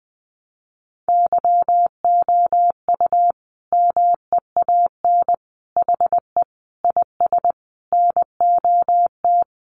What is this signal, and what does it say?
Morse code, heard as a keyed tone, about 20 wpm, 700 Hz. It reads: YOU MEAN HE IS NOT
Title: Sentences from Top 1200 Words 20wpm